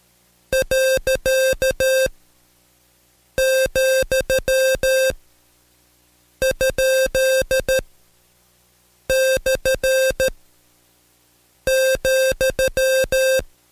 Morse code, heard as a keyed tone, about 13 wpm, spaced as for 5 wpm, 530 Hz. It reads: .,?/,
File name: Code Quick / AudioDownld2 / Seg107-Day13-DayDrill 13